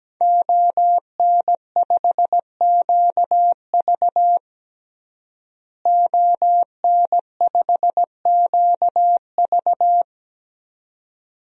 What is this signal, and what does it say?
Morse code, heard as a keyed tone, about 17 wpm, 700 Hz. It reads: ON5QV ON5QV
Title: DXCC-100 Call Signs - 5 Characters - Repeated 2x - 17wpm